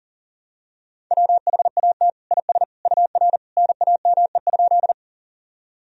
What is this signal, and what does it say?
Morse code, heard as a keyed tone, about 40 wpm, 700 Hz. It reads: WHAT IS UR NAME?
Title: Sentences from Top 200 Words and Abbreviations 40wpm